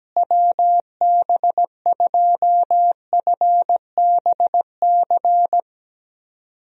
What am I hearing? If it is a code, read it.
Morse code, heard as a keyed tone, about 17 wpm, 700 Hz. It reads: WB2FBC
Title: USA Call Signs - 6 Characters - Repeated 3x - 17wpm